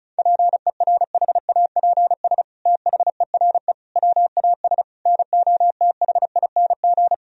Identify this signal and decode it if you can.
Morse code, heard as a keyed tone, about 36 wpm, 700 Hz. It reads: PERHAPS THERE WAS NOTHING
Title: Sentences from Top 1000 Words - Review Part 4 - 35wpm